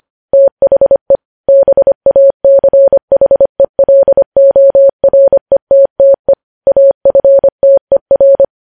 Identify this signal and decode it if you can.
Morse code, heard as a keyed tone, about 25 wpm, 550 Hz. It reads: THE BACHELORETTE AFTER
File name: podcast / News headlines-1x-25x25